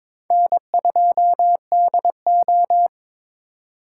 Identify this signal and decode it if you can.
Morse code, heard as a keyed tone, about 26 wpm, 700 Hz. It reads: N2DO
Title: Speed Racing - USA 1x2 Call Signs 22wpm